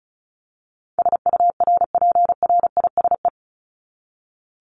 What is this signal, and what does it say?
Morse code, heard as a keyed tone, about 34 wpm, 700 Hz. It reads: SURPRISE
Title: Top 1751-2000 Words 35wpm